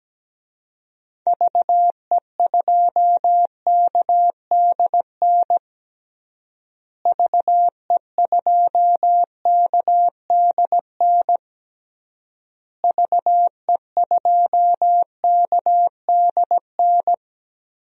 Morse code, heard as a keyed tone, about 17 wpm, 700 Hz. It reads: VE2KDN VE2KDN VE2KDN
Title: DXCC-100 Call Signs - 6 Characters - Repeated 3x - 17wpm